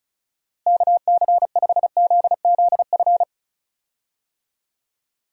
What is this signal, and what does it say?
Morse code, heard as a keyed tone, about 34 wpm, 700 Hz. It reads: KC5ZZF
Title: USA Call Signs - 6 Characters - Repeated 3x - 35wpm